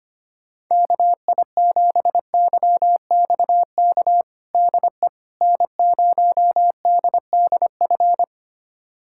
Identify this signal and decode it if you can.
Morse code, heard as a keyed tone, about 25 wpm, 700 Hz. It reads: KI7YXK DE N0DDF
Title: USA Callsign DE Callsign 25wpm